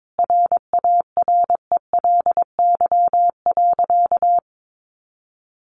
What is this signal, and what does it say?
Morse code, heard as a keyed tone, about 22 wpm, 700 Hz. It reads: RARELY.
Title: Sentences from Top 1400 Words 22wpm